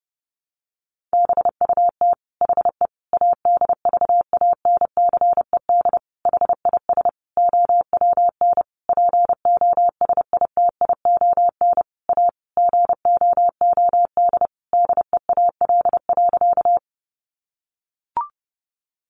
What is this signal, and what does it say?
Morse code, heard as a keyed tone, about 30 wpm, 700 Hz. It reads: BUT HE ADVANCED HIS OWN POSITION A GOOD DEAL.
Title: Sentences from Top 1200 Words 30wpm